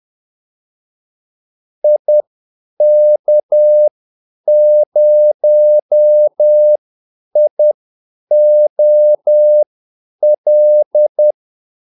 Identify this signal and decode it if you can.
Morse code, heard as a keyed tone, about 10 wpm, spaced as for 8 wpm, 600 Hz. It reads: IK0IOL
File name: Callsigns-10-8wpm